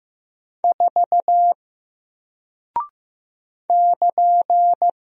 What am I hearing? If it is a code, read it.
Morse code, heard as a keyed tone, about 15 wpm, 700 Hz. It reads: 4 <KN>
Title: Single Character (Letter:Number:Punctuation:Prosign) Alt 15wpm